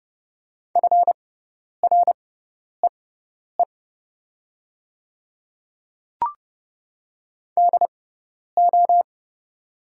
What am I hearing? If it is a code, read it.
Morse code, heard as a keyed tone, about 29 wpm, spaced as for 10 wpm, 700 Hz. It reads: FREE DO
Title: 6x Character Spacing - Top 500 Words - 30wpm